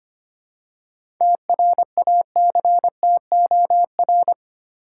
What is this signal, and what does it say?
Morse code, heard as a keyed tone, about 25 wpm, 700 Hz. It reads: TRACTOR
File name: practice / 25wpm / Easy Words with Hints - 25wpm